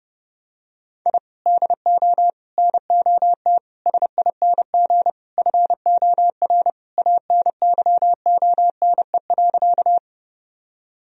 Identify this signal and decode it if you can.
Morse code, heard as a keyed tone, about 30 wpm, 700 Hz. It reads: I DO NOT SING FOR ANYONE.
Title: Sentences from Top 1000 Words - Review Part 2 - 30wpm